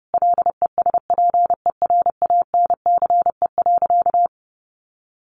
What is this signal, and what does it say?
Morse code, heard as a keyed tone, about 30 wpm, 700 Hz. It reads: LESPERANCE.
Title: A Sound Of Thunder 30wpm